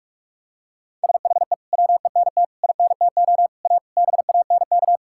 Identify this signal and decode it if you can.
Morse code, heard as a keyed tone, about 46 wpm, 700 Hz. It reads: SHE WENT INTO A BANK
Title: Sentences from Top 1000 Words - Review Part 6 - 45wpm